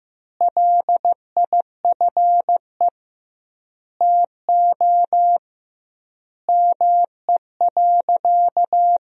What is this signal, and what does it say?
Morse code, heard as a keyed tone, about 15 wpm, 700 Hz. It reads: LIFE TO ME.
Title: 2x Word-Spacing - Sentences from Top 500 Words - 15wpm